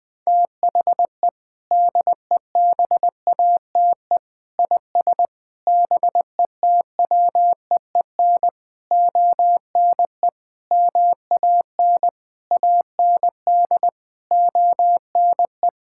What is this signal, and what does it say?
Morse code, heard as a keyed tone, about 20 wpm, 700 Hz. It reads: THE DEBATE IS BETWEEN ONE MAN AND ONE
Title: Sentences with 6-Word Phrases - Spoken Beginning - 20wpm